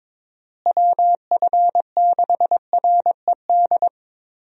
Morse code, heard as a keyed tone, about 22 wpm, 700 Hz. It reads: WF6RED
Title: USA Call Signs - 6 Characters - Repeated 3x - 22wpm